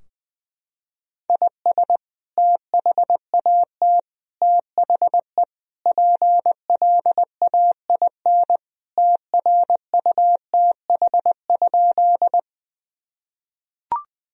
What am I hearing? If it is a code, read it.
Morse code, heard as a keyed tone, about 20 wpm, 700 Hz. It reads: IS THAT THE PLAIN TRUTH?
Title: Sentences from Top 1000 Words - Review Part 2 - 20wpm